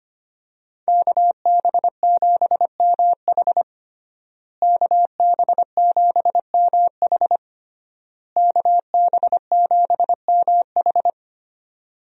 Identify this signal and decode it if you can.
Morse code, heard as a keyed tone, about 25 wpm, 700 Hz. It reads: KB7MH KB7MH KB7MH
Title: USA Call Signs - 5 Characters - Repeated 3x - 25wpm